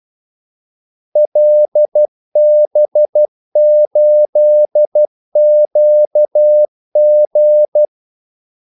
Morse code, heard as a keyed tone, about 12 wpm, 600 Hz. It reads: LB8QG
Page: 2 ekstra ordmellemrum mellem kaldesignalerne